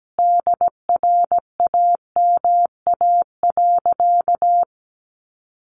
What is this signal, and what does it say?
Morse code, heard as a keyed tone, about 17 wpm, 700 Hz. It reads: DRAMA.
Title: Sentences from Top 1500 Words 17wpm